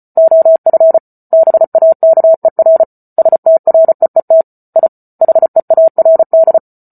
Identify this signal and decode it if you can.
Morse code, heard as a keyed tone, about 34 wpm, 650 Hz. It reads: OF BAKER STREET I HEARD